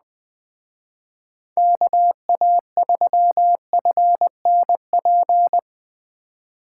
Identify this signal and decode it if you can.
Morse code, heard as a keyed tone, about 20 wpm, 700 Hz. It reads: KA3FNP